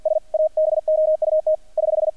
Morse code, heard as a keyed tone, about 50 wpm, 620 Hz, with radio noise. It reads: S ABOUT <HH>
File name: archived / CRSnet